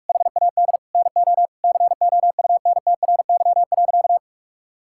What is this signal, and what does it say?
Morse code, heard as a keyed tone, about 41 wpm, 700 Hz. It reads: HAD NO COUNTRY.
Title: Sentences from Top 1000 Words - Review Part 5 - 45wpm